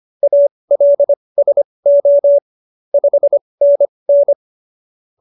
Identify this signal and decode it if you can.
Morse code, heard as a keyed tone, about 25 wpm, spaced as for 20 wpm, 570 Hz. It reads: ALSO 5NN